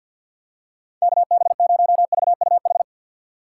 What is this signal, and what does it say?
Morse code, heard as a keyed tone, about 47 wpm, 690 Hz. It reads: KB0VUH
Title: USA Call Signs - 6 Characters - 3x Character Spacing - 50wpm